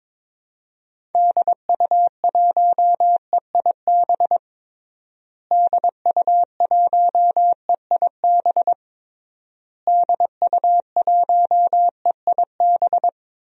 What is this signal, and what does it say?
Morse code, heard as a keyed tone, about 22 wpm, 700 Hz. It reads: DU1EIB DU1EIB DU1EIB
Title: DXCC-100 Call Signs - 6 Characters - Repeated 3x - 22wpm